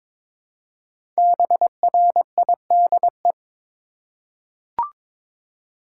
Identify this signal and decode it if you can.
Morse code, heard as a keyed tone, about 22 wpm, 700 Hz. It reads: BRIDE
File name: 5 Letter Words - First 3 Letters Given - 22wpm